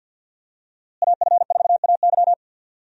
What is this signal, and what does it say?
Morse code, heard as a keyed tone, about 52 wpm, 700 Hz. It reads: AF4AY
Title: USA Call Signs - 5 Characters - Repeated 3x - 50wpm